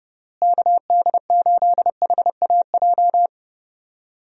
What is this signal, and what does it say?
Morse code, heard as a keyed tone, about 30 wpm, 700 Hz. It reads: KD8HAJ
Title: USA Call Signs - 6 Characters - Repeated 3x - 30wpm